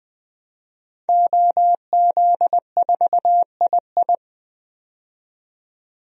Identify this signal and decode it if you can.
Morse code, heard as a keyed tone, about 20 wpm, 700 Hz. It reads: OZ4II
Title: DXCC-100 Call Signs - 5 Characters - 20wpm